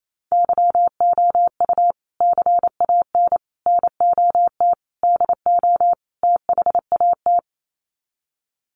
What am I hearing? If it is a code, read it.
Morse code, heard as a keyed tone, about 28 wpm, 700 Hz. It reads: YOU CAN NOT DO THAT
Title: Sets of 5 Words 28wpm